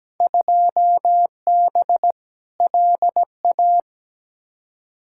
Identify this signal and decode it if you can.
Morse code, heard as a keyed tone, about 17 wpm, 700 Hz. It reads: 2B LA
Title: Field Day Exchanges Encore - 17wpm